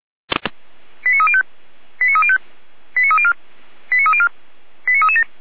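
Five-tone selective call (ZVEI), heard as quick runs of five tones, tones of about 70 ms. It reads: 89305 89305 89304 89304 89337
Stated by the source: Five-tone ZVEI selective call received by SDR, from feuerwehr-orlamuende.org audio archive (fire brigade Orlamünde).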